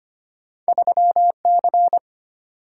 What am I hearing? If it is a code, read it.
Morse code, heard as a keyed tone, about 25 wpm, 700 Hz. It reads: 3C
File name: Two Letter-Number Combinations - 25wpm